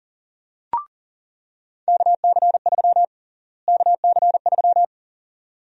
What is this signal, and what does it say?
Morse code, heard as a keyed tone, about 37 wpm, 700 Hz. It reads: KC3 KC3
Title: USA Call Sign Prefixes - Repeated 2x - 40wpm